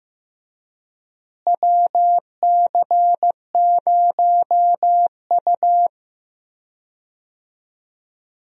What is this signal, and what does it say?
Morse code, heard as a keyed tone, about 15 wpm, 700 Hz. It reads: WC0U
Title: USA Call Signs - 4 Characters - Repeated 2x - 15wpm